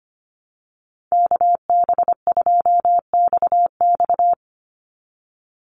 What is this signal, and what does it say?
Morse code, heard as a keyed tone, about 25 wpm, 700 Hz. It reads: KB2XX
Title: NA SOTA Chaser Callsigns 25wpm